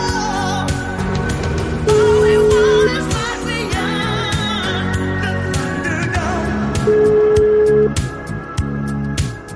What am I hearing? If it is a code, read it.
Ringback tone (Europe, 425 Hz), with noise.